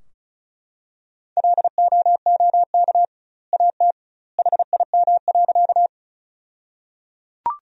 Morse code, heard as a keyed tone, about 34 wpm, 700 Hz. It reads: LOOK AT HIM.
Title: 2x Word-Spacing - Sentences from Top 100 Words - 35wpm